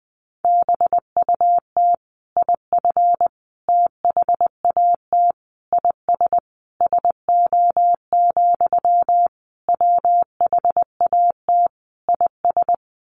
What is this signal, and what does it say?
Morse code, heard as a keyed tone, about 20 wpm, 700 Hz. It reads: BUT IF THAT IS SO, WHAT IS
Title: Sentences from Top 200 Words 20wpm